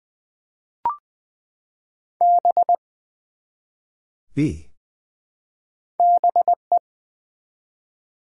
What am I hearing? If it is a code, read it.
Morse code, heard as a keyed tone, about 20 wpm, 690 Hz. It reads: B BE
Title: Build A Word - 1 Letter At A Time - 3-Letter Words - 20wpm